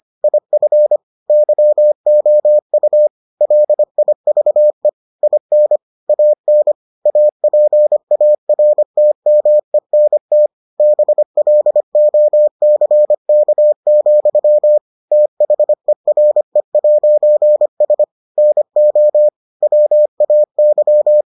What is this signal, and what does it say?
Morse code, heard as a keyed tone, about 25 wpm, 600 Hz. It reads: IF YOU LIVE IN AN APARTMENT BLOCK, THERE'S NO WAY